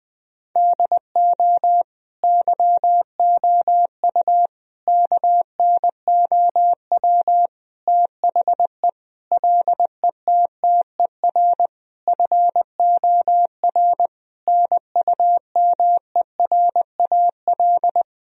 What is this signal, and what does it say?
Morse code, heard as a keyed tone, about 20 wpm, 700 Hz. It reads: DO YOU KNOW THE LETTER FOR NUMERAL
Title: Sentences from Top 1000 Words - Review Part 2 - 20wpm